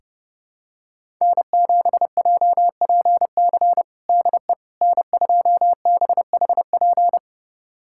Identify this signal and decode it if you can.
Morse code, heard as a keyed tone, about 30 wpm, 700 Hz. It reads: N7JPC DE N2BHP
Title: USA Callsign DE Callsign - 5 Characters - 30wpm